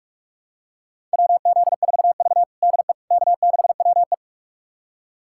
Morse code, heard as a keyed tone, about 46 wpm, 700 Hz. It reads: WZ4V DE K6WE
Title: USA Callsign DE Callsign - 4 Characters - 45wpm